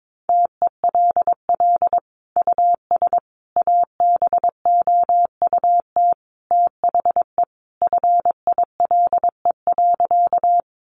Morse code, heard as a keyed tone, about 22 wpm, 700 Hz. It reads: TELL US ABOUT THE FILE.
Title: Sentences from Top 1200 Words 22wpm